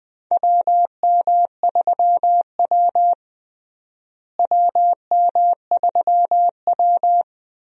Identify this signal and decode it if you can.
Morse code, heard as a keyed tone, about 20 wpm, 700 Hz. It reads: WM3W WM3W
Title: USA Call Signs - 4 Characters - Repeated 2x - 20wpm